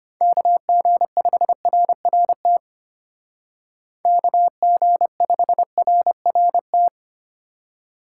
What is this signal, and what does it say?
Morse code, heard as a keyed tone, about 32 wpm, 700 Hz. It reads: KG5RRT KG5RRT